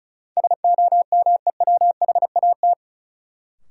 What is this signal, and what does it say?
Morse code, heard as a keyed tone, about 34 wpm, 700 Hz. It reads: SOMEWHAT
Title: Sentences from Top 1300 Words - New Word Drill Alt - 35wpm